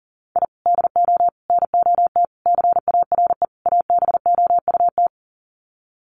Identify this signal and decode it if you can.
Morse code, heard as a keyed tone, about 40 wpm, 700 Hz. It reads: I DO NOT CARE ABOUT
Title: Sets of 5 Words Encore2 40wpm